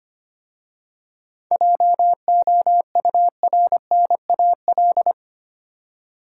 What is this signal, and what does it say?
Morse code, heard as a keyed tone, about 25 wpm, 700 Hz. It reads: JOURNAL
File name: practice / 25wpm / Difficult Words with Hints - 25wpm